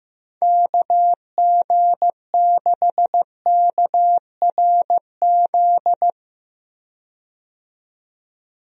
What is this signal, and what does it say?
Morse code, heard as a keyed tone, about 15 wpm, 700 Hz. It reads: KG6KRZ
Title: USA Call Signs - 6 Characters - 15wpm